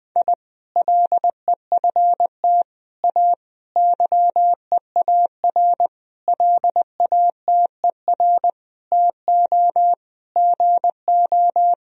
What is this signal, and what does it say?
Morse code, heard as a keyed tone, about 20 wpm, 700 Hz. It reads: I LEFT A YEAR LATER TO GO
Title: Sentences from Top 1000 Words - Review Part 4 - 20wpm